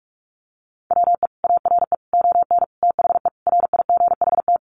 Morse code, heard as a keyed tone, about 46 wpm, 700 Hz. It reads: WE ARE ON THE RIGHT
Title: Sentences from Top 1100 Words 45wpm